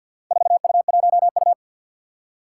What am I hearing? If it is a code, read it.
Morse code, heard as a keyed tone, about 51 wpm, 700 Hz. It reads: 4U1U
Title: Four Challenging Letter-Number Combinations 50wpm